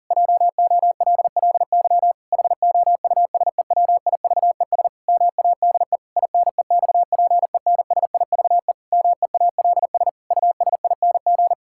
Morse code, heard as a keyed tone, about 37 wpm, 700 Hz. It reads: JOLLY HOUSEWIVES MADE INEXPENSIVE MEALS USING
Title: Pangram Challenge v2 - 40wpm